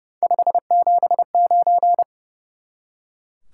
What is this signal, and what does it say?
Morse code, heard as a keyed tone, about 31 wpm, 700 Hz. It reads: 579
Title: Top 180 QSO Elements 30wpm